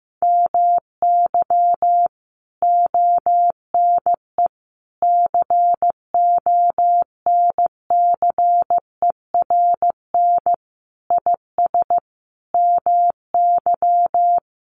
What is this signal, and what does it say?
Morse code, heard as a keyed tone, about 15 wpm, 700 Hz. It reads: MY ONE CONCERN IS MY
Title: Sentences from Top 1200 Words 15wpm